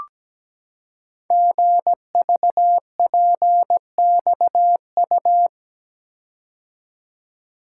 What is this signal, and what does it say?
Morse code, heard as a keyed tone, about 17 wpm, 700 Hz. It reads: GVPXU
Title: Code Groups - 5 Letters - 17wpm